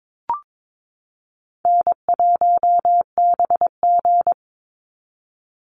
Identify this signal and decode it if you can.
Morse code, heard as a keyed tone, about 22 wpm, 700 Hz. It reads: N1BG
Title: Easy Call Signs Repeated 2x - 22wpm